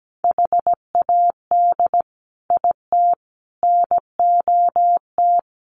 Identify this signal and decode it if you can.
Morse code, heard as a keyed tone, about 17 wpm, 700 Hz. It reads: HAD IT NOT
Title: Sets of 5 Words Encore2 17wpm